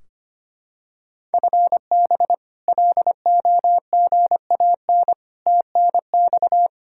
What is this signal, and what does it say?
Morse code, heard as a keyed tone, about 25 wpm, 700 Hz. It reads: FB LOGAN TNX
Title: K1USN SST QSO Practice 25wpm